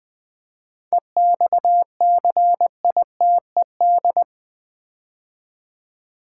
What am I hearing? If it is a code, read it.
Morse code, heard as a keyed tone, about 20 wpm, 700 Hz. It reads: EXCITED
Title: 7 Letter Words - First 3 Letters Given - 20wpm